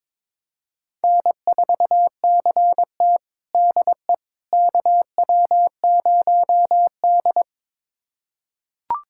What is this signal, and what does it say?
Morse code, heard as a keyed tone, about 22 wpm, 700 Hz. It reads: N4CT DE KW0D
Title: USA Callsign DE Callsign - 4 Characters - 22wpm